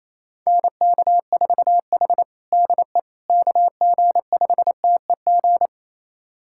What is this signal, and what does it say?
Morse code, heard as a keyed tone, about 28 wpm, 700 Hz. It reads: NK4H DE KG5TEG
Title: USA Callsign DE Callsign 28wpm